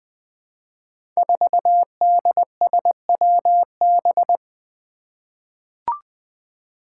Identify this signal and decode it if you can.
Morse code, heard as a keyed tone, about 20 wpm, 700 Hz. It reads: 4DSWB
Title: Five Challenging Letter-Number Combinations 20wpm